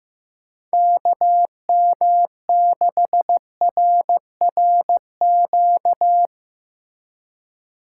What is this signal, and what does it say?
Morse code, heard as a keyed tone, about 17 wpm, 700 Hz. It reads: KM6RRQ
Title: Speed Racing - USA Call Signs 15wpm